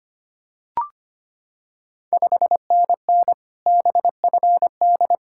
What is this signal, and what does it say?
Morse code, heard as a keyed tone, about 25 wpm, 690 Hz. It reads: 5NN BFD
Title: Ontario QSO Party Exchange 25wpm